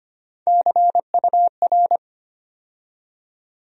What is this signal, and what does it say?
Morse code, heard as a keyed tone, about 25 wpm, 700 Hz. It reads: CUR
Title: USA Call Sign Postfixes - 3 Letters 25wpm